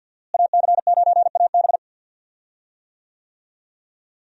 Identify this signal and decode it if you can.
Morse code, heard as a keyed tone, about 52 wpm, 700 Hz. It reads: AC9AB
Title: USA Call Signs - 5 Characters - Repeated 2x - 50wpm